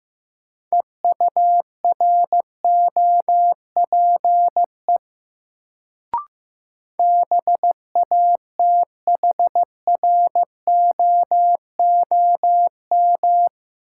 Morse code, heard as a keyed tone, about 15 wpm, 700 Hz. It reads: EUROPE BATHROOM
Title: Sets of 5 Words Encore - New Word Drill - 15wpm